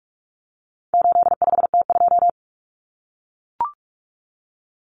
Morse code, heard as a keyed tone, about 44 wpm, 700 Hz. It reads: 85T2
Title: Common Cut Numbers - 4-Digit 45wpm